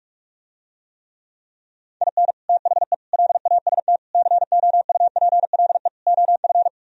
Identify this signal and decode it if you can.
Morse code, heard as a keyed tone, about 44 wpm, 700 Hz. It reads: IN THE LAST COUPLE OF